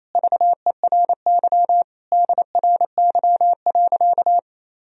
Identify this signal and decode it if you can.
Morse code, heard as a keyed tone, about 28 wpm, 700 Hz. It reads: VERY DRY.
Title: Sentences from Top 1000 Words - Review Part 3 - 28wpm